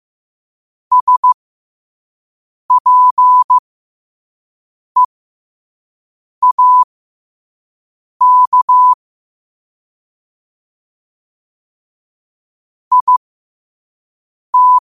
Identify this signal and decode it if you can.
Morse code, heard as a keyed tone, about 15 wpm, spaced as for 5 wpm, 1000 Hz. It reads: SPEAK IT
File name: qotdfiles / 5wpm